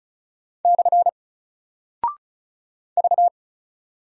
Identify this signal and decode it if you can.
Morse code, heard as a keyed tone, about 36 wpm, 700 Hz. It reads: / V
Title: Single Character (Letter:Number:Punctuation:Prosign) Alt 35wpm